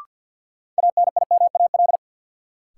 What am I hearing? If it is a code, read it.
Morse code, heard as a keyed tone, about 52 wpm, 700 Hz. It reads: ANIMAL
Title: Sentences from Top 1000 Words - Review Part 1 - New Word Drill - 50wpm